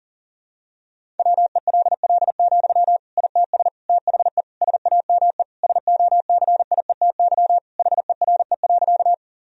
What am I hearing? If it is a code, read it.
Morse code, heard as a keyed tone, about 37 wpm, 700 Hz. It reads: WELL, ITS THE SAME SOCIETY HERE.
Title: Sentences from Top 1000 Words - Review Part 4 - 40wpm